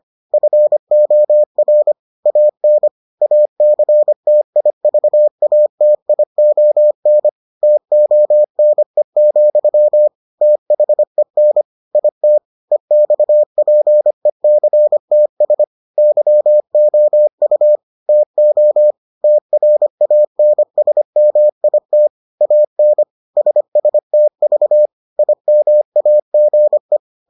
Morse code, heard as a keyed tone, about 25 wpm, 600 Hz. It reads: FOR AN ACTIVATION TONE, THEN IT EXPECTS YOU TO TRANSMIT AN SSTV IMAGE